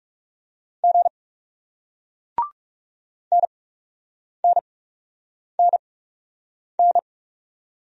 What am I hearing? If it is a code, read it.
Morse code, heard as a keyed tone, about 40 wpm, spaced as for 8 wpm, 700 Hz. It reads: G NNNN